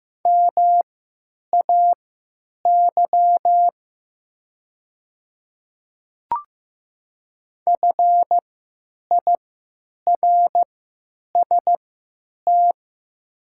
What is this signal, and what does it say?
Morse code, heard as a keyed tone, about 15 wpm, spaced as for 9 wpm, 700 Hz. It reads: MAY FIRST
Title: 3x Character Spacing - Top 100 Words - 15wpm